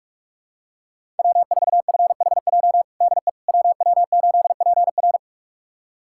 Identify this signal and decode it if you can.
Morse code, heard as a keyed tone, about 45 wpm, 700 Hz. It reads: W4FHJ DE WW8PR
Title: USA Callsign DE Callsign - 5 Characters - 45wpm